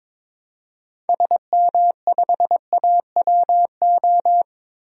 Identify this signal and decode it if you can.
Morse code, heard as a keyed tone, about 22 wpm, 700 Hz. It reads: SM5AWO
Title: DXCC-100 Call Signs - 6 Characters - 22wpm